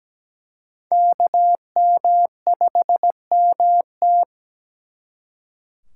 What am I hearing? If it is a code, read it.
Morse code, heard as a keyed tone, about 20 wpm, 700 Hz. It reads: KM5MT
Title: Speed Racing - USA Call Signs 17wpm